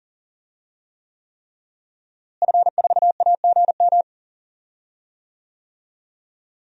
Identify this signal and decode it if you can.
Morse code, heard as a keyed tone, about 40 wpm, 700 Hz. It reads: F4AGM
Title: DXCC-100 Call Signs - 5 Characters - Repeated 3x - 40wpm